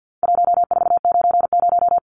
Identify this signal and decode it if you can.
Morse code, heard as a keyed tone, about 50 wpm, 700 Hz. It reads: 1490
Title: Numbers - 4-Digit 50wpm